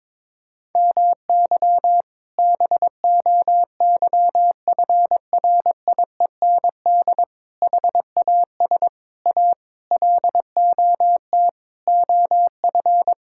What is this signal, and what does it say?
Morse code, heard as a keyed tone, about 22 wpm, 700 Hz. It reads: MY BOYFRIEND HAS A LOT OF
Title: Sentences from Top 1000 Words - Review Part 5 - 22wpm